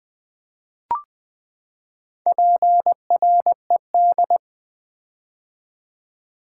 Morse code, heard as a keyed tone, about 20 wpm, 700 Hz. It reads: PRED
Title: Build A Word - 4 Letters At A Time - 11-Letter Words - 20wpm